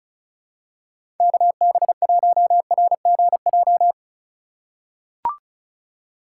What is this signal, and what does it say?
Morse code, heard as a keyed tone, about 36 wpm, 700 Hz. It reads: KB1RGJ
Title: USA Call Signs - 6 Characters - 2x Character Spacing - 35wpm